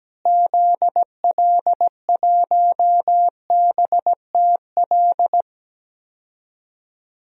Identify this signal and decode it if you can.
Morse code, heard as a keyed tone, about 17 wpm, 700 Hz. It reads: ZL1BTL
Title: DXCC-100 Call Signs - 6 Characters - Repeated 3x - 17wpm